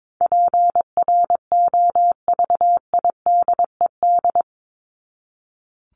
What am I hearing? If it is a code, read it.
Morse code, heard as a keyed tone, about 22 wpm, 700 Hz. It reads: PROVIDED
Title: Top 1751-2000 Words 22wpm